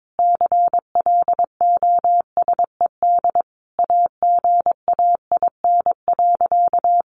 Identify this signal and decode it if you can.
Morse code, heard as a keyed tone, about 22 wpm, 700 Hz. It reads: CLOSED AGAIN.
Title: Sentences from Top 1400 Words 22wpm